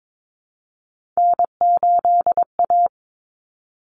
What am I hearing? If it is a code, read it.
Morse code, heard as a keyed tone, about 22 wpm, 700 Hz. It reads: N8A
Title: Common Cut Numbers - 3-Digit 22wpm